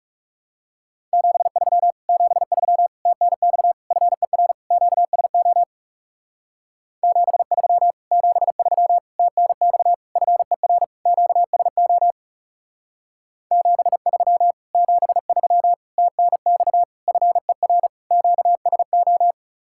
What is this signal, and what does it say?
Morse code, heard as a keyed tone, about 36 wpm, 700 Hz. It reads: 73 73 TNX FER QSO 73 73 TNX FER QSO 73 73 TNX FER QSO
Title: Speed-Racing Common QSO Phrases - 30wpm